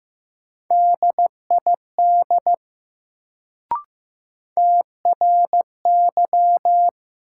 Morse code, heard as a keyed tone, about 15 wpm, 700 Hz. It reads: DID TRY
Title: Sets of 2 Words - New Word Drill - 15wpm